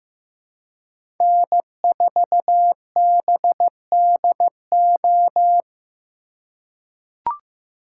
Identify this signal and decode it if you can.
Morse code, heard as a keyed tone, about 15 wpm, 700 Hz. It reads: N4BDO
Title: USA Call Signs - 5 Characters - 4x Character Spacing - 15wpm